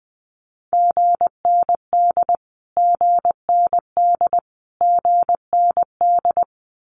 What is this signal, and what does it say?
Morse code, heard as a keyed tone, about 20 wpm, 700 Hz. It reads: GND GND GND
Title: Abbreviations Repeated 3x - 20wpm